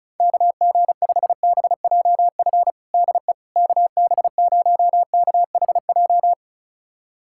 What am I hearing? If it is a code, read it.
Morse code, heard as a keyed tone, about 36 wpm, 700 Hz. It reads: KG5BJF DE KB0KHJ
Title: USA Callsign DE Callsign - 6 Characters - 35wpm